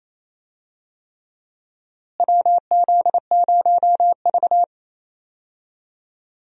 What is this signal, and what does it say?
Morse code, heard as a keyed tone, about 28 wpm, 700 Hz. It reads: WZ0V
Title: USA Call Signs - 4 Characters - Repeated 3x - 28wpm